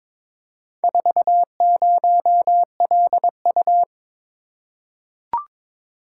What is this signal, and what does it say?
Morse code, heard as a keyed tone, about 22 wpm, 700 Hz. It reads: 40LU
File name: practice / 22wpm / Four Letter-Number Combinations - 22wpm